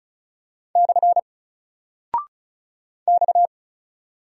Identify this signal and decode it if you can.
Morse code, heard as a keyed tone, about 34 wpm, 700 Hz. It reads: / X
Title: Single Character (Letter:Number:Punctuation:Prosign) 35wpm